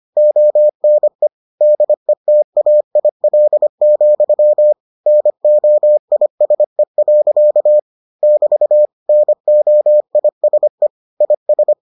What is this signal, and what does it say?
Morse code, heard as a keyed tone, about 25 wpm, 600 Hz. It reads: ONE DETAIL, NOISE. <BT> NOISE IS